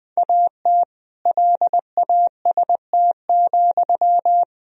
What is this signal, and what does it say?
Morse code, heard as a keyed tone, about 20 wpm, 700 Hz. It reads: AT LAST,
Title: Sentences from Top 1000 Words - Review Part 2 - 20wpm